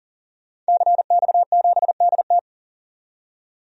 Morse code, heard as a keyed tone, about 40 wpm, 700 Hz. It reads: CX7DT
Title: DXCC-100 Call Signs - 5 Characters - 40wpm